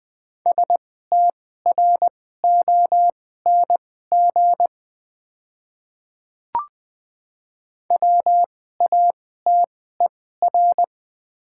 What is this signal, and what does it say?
Morse code, heard as a keyed tone, about 20 wpm, spaced as for 15 wpm, 700 Hz. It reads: STRONG WATER
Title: 2x Character Spacing - Top 500 Words - 20wpm